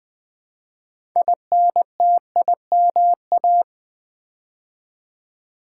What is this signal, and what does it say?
Morse code, heard as a keyed tone, about 20 wpm, 700 Hz. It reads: INTIMA
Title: Build A Word - 2 Letters At A Time - 8-Letter Words - 20wpm